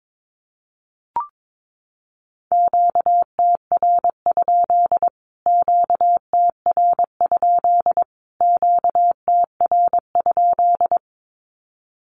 Morse code, heard as a keyed tone, about 22 wpm, 700 Hz. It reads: QTR? QTR? QTR?
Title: Q-Codes Repeated 3x 22wpm